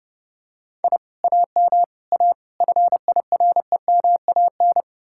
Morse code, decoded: I AM A FIREMAN

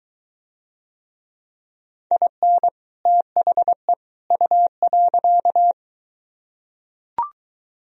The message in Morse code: IN THE U.

